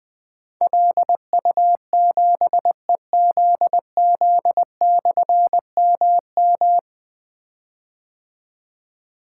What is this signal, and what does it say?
Morse code, heard as a keyed tone, about 20 wpm, 700 Hz. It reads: LU7EZZ/MM
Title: Difficult Call Signs 20wpm